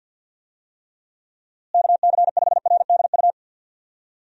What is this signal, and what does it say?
Morse code, heard as a keyed tone, about 50 wpm, 700 Hz. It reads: KC5RDU
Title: USA Call Signs - 6 Characters - Repeated 3x - 50wpm